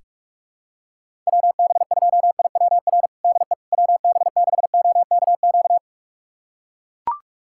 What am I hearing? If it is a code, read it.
Morse code, heard as a keyed tone, about 41 wpm, 700 Hz. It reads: WB2IWR DE WB6OKQ
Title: USA Callsign DE Callsign 45wpm